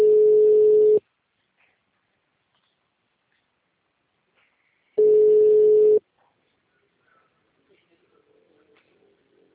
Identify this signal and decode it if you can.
Ringback tone (Europe, 425 Hz), with speech.